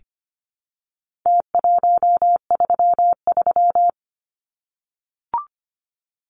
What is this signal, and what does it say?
Morse code, heard as a keyed tone, about 25 wpm, 700 Hz. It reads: T133
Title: Numbers - Contest Style - 4-Digit 25wpm